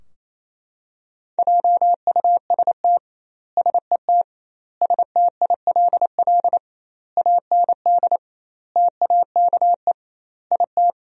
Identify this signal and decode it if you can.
Morse code, heard as a keyed tone, about 27 wpm, 700 Hz. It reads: JUST SET STILL AND TAKE IT
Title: 2x Word-Spacing - Sentences from Top 300 Words - 28wpm